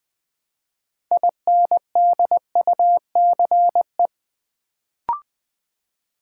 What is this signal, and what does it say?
Morse code, heard as a keyed tone, about 20 wpm, 700 Hz. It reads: INDUCE
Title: Build A Word - 2 Letters At A Time - 6-Letter Words - 20wpm